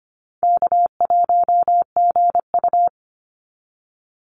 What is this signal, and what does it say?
Morse code, heard as a keyed tone, about 25 wpm, 700 Hz. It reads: K1GU
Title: Call Signs Repeated 2x - 25wpm